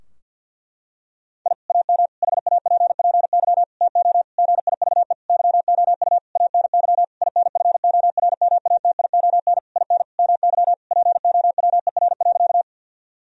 Morse code, heard as a keyed tone, about 50 wpm, 700 Hz. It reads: I AM HAPPY TO GIVE YOU ANY INFORMATION IN MY POWER.
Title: Sentences from Top 1000 Words - Review Part 3 - 50wpm